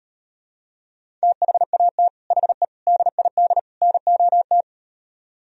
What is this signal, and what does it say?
Morse code, heard as a keyed tone, about 38 wpm, 700 Hz. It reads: THAT HE DID NOT